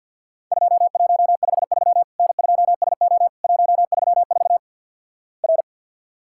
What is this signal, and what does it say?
Morse code, heard as a keyed tone, about 49 wpm, 700 Hz. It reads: 2153 N2SO 134 R
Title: Worked All Europe DX Contest - QTC Exchange - 50wpm